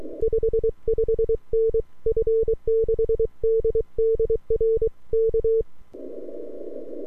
Morse code, heard as a keyed tone, about 23 wpm, 450 Hz. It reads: 55N F6DDR K
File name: n6tr / f6ddr